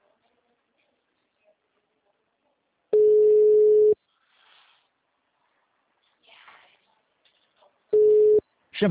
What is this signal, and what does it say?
Ringback tone (Europe, 425 Hz), with speech.